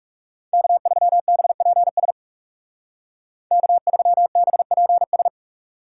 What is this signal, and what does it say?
Morse code, heard as a keyed tone, about 39 wpm, 700 Hz. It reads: K3BPS K3BPS